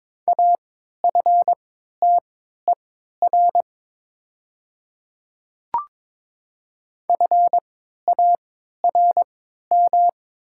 Morse code, heard as a keyed tone, about 21 wpm, spaced as for 12 wpm, 700 Hz. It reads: AFTER FARM
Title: 3x Character Spacing - Top 500 Words - 22wpm